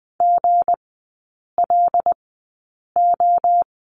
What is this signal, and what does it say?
Morse code, heard as a keyed tone, about 20 wpm, spaced as for 8 wpm, 690 Hz. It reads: GLO
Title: 159 Lesson 26e - Letter g - Three Characters 20wpm